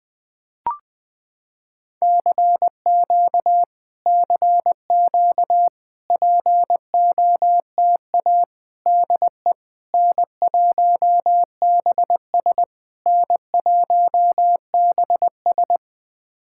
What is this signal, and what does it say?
Morse code, heard as a keyed tone, about 20 wpm, 700 Hz. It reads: CQ CQ POTA DE N1BS N1BS
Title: POTA QSO Practice - 20wpm